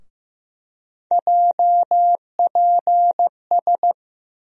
Morse code, heard as a keyed tone, about 15 wpm, 700 Hz. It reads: JPS
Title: USA Call Sign Postfixes - 3 Letters 15wpm